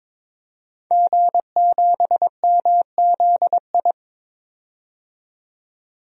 Morse code, heard as a keyed tone, about 21 wpm, 700 Hz. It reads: G7MZI
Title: DXCC-100 Call Signs - 5 Characters - 22wpm